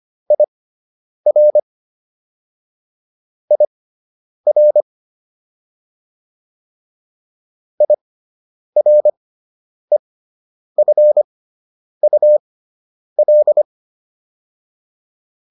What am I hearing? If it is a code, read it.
Morse code, heard as a keyed tone, about 25 wpm, spaced as for 9 wpm, 600 Hz. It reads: IR IR IREFUL